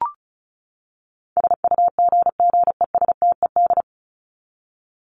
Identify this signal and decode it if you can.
Morse code, heard as a keyed tone, about 34 wpm, 700 Hz. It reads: SUGGESTED
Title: ED Suffix - Top 100 Words with Base Words 35wpm